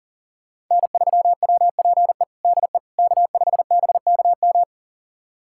Morse code, heard as a keyed tone, about 37 wpm, 700 Hz. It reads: N3WPE DE K5BKM
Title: USA Callsign DE Callsign - 5 Characters - 40wpm